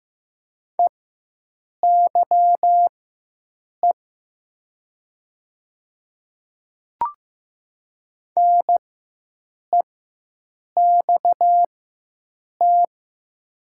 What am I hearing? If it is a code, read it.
Morse code, heard as a keyed tone, about 15 wpm, spaced as for 7 wpm, 700 Hz. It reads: EYE NEXT
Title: 4x Character Spacing - Top 300 Words - 15wpm